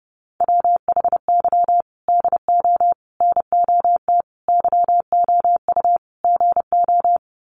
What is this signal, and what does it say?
Morse code, heard as a keyed tone, about 30 wpm, 700 Hz. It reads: WHY DO NOT YOU GO
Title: Sets of 5 Words Encore 30wpm